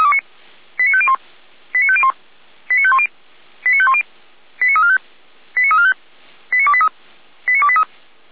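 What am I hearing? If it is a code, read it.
Five-tone selective call (ZVEI), heard as quick runs of five tones, tones of about 70 ms. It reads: 89501 89501 89510 89510 89345 89345 89283 89283
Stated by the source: ZVEI five-tone call received on SDR.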